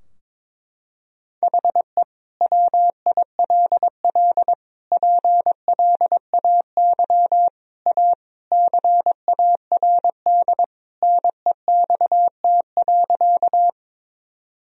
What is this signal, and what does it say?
Morse code, heard as a keyed tone, about 22 wpm, 700 Hz. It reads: HE WILL PLAY A CARD NEXT.
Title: Sentences from Top 1000 Words - Review Part 4 - 22wpm